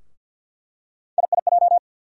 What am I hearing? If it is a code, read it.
Morse code, heard as a keyed tone, about 48 wpm, 700 Hz. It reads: IIJ
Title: USA Call Sign Postfixes - 3 Letters 50wpm